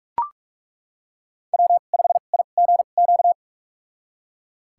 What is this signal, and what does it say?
Morse code, heard as a keyed tone, about 41 wpm, spaced as for 31 wpm, 700 Hz. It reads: W5IGQ
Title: USA Call Signs - 5 Characters - 2x Character Spacing - 45wpm